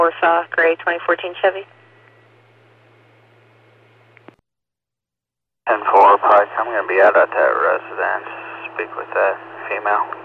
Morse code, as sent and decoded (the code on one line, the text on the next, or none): none